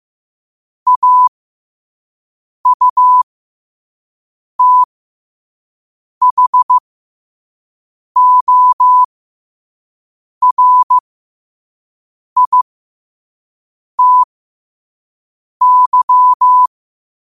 Morse code, heard as a keyed tone, about 15 wpm, spaced as for 5 wpm, 1000 Hz. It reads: AUTHORITY
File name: qotdfiles / 5wpm